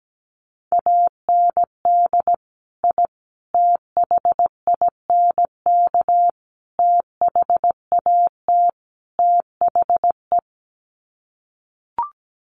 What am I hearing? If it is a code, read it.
Morse code, heard as a keyed tone, about 17 wpm, 700 Hz. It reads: AND I THINK THAT THE
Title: Sets of 5 Words 17wpm